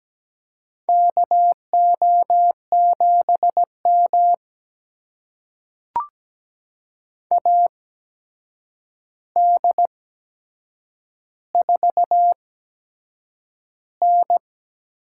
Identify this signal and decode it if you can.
Morse code, heard as a keyed tone, about 17 wpm, spaced as for 7 wpm, 700 Hz. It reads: KO7M AD4N
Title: USA Call Signs - 4 Characters - 8x Character Spacing - 17wpm